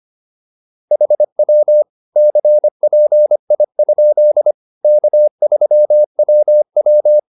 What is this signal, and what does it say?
Morse code, heard as a keyed tone, about 25 wpm, 600 Hz. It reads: HW CPI? K3WW